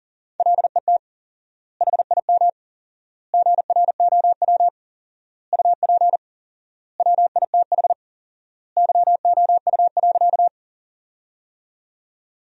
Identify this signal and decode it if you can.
Morse code, heard as a keyed tone, about 37 wpm, 700 Hz. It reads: LET HIM GROW UP WITH YOU.
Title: 4x Word-Spacing - Sentences from Top 300 Words - 40wpm